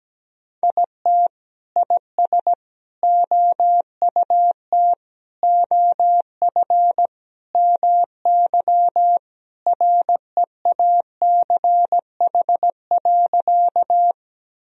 Morse code, heard as a keyed tone, about 17 wpm, 700 Hz. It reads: IT IS OUT OF MY REACH.
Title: Sentences from Top 1000 Words - Review Part 2 - 17wpm